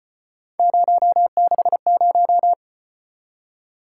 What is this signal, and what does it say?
Morse code, heard as a keyed tone, about 35 wpm, 700 Hz. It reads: 060